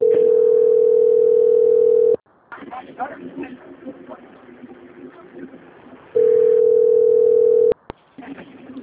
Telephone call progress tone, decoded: ringback tone

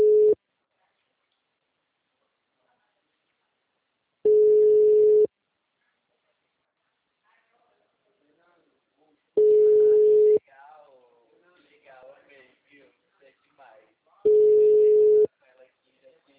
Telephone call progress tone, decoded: ringback tone

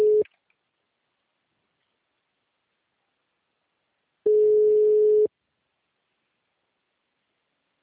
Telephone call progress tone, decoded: ringback tone

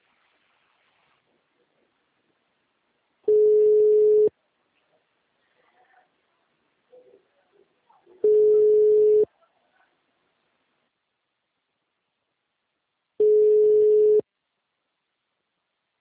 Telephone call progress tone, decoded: ringback tone

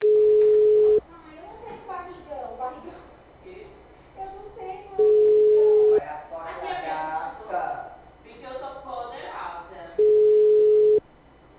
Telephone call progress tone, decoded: ringback tone

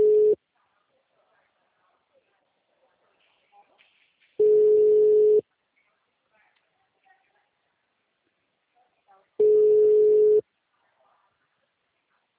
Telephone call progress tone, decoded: ringback tone

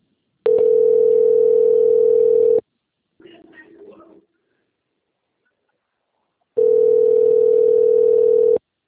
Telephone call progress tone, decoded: ringback tone

